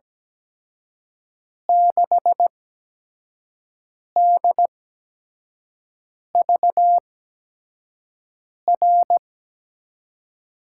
Morse code, heard as a keyed tone, about 17 wpm, spaced as for 5 wpm, 700 Hz. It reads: 6DVR
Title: USA Call Signs - 5 Characters - 8x Character Spacing - 17wpm